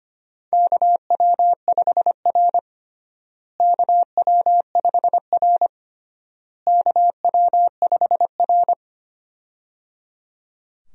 Morse code, heard as a keyed tone, about 25 wpm, 700 Hz. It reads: KW5R KW5R KW5R
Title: USA Call Signs - 4 Characters - Repeated 3x - 25wpm